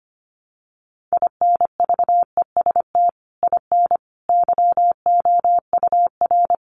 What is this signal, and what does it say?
Morse code, heard as a keyed tone, about 25 wpm, 700 Hz. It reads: INVEST IN YOUR
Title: Sentences from Top 1100 Words 25wpm